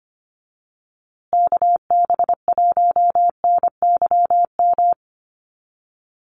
Morse code, heard as a keyed tone, about 25 wpm, 700 Hz. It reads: KB1NYM
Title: USA Call Signs Repeated 2x 25wpm